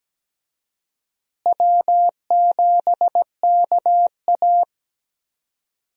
Morse code, heard as a keyed tone, about 17 wpm, 700 Hz. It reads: W7KA
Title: USA Call Signs - 4 Characters - Repeated 2x - 17wpm